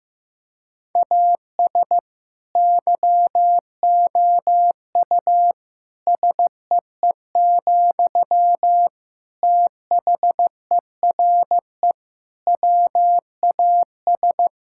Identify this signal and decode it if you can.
Morse code, heard as a keyed tone, about 15 wpm, 700 Hz. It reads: AS YOU SEE, THERE WAS
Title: Sentences from Top 1000 Words - Review Part 6 - 15wpm